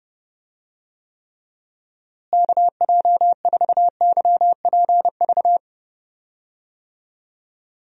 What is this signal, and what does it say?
Morse code, heard as a keyed tone, about 30 wpm, 700 Hz. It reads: KJ4YPV
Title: USA Call Signs - 6 Characters - Repeated 3x - 30wpm